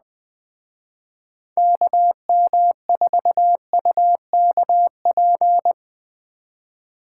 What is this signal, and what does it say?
Morse code, heard as a keyed tone, about 20 wpm, 700 Hz. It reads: KM4UKP